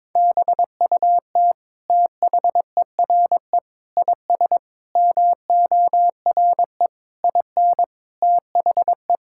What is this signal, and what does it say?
Morse code, heard as a keyed tone, about 22 wpm, 700 Hz. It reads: BUT THERE IS MORE IN THE
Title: Sentences from Top 1000 Words - Review Part 5 - 22wpm